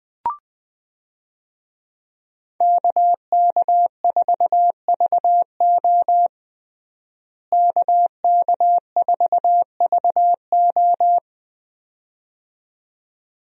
Morse code, heard as a keyed tone, about 20 wpm, 700 Hz. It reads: KK4VO KK4VO
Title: USA Call Signs - 5 Characters - Repeated 2x - 20wpm